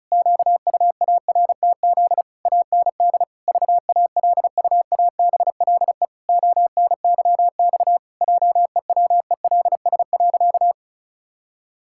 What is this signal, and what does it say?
Morse code, heard as a keyed tone, about 34 wpm, 700 Hz. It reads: QUARTZ AND VALUABLE ONYX JEWELS.
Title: Pangram Challenge v2 - 35wpm